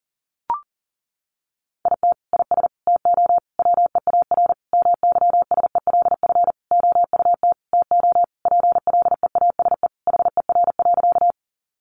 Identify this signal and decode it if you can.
Morse code, heard as a keyed tone, about 37 wpm, 700 Hz. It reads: IT IS TO WEAR MYSELF OUT TO PLEASE HER.
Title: Sentences from Top 1500 Words 40wpm